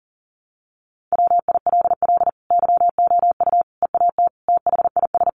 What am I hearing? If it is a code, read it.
Morse code, heard as a keyed tone, about 40 wpm, 700 Hz. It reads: WILL YOU EAT THIS
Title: Sentences from Top 1300 Words 40wpm